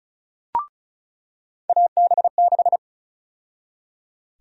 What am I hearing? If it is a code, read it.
Morse code, heard as a keyed tone, about 34 wpm, 700 Hz. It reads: AB6
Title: USA Call Sign Prefixes - 35wpm